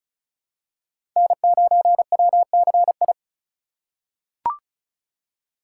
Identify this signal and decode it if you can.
Morse code, heard as a keyed tone, about 34 wpm, 700 Hz. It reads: N9WCI
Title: USA Call Signs - 2x Character Spacing - 35wpm